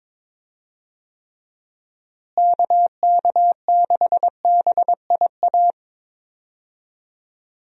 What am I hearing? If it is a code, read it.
Morse code, heard as a keyed tone, about 22 wpm, 700 Hz. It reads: KK6BIA
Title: USA Call Signs - 6 Characters - Repeated 3x - 22wpm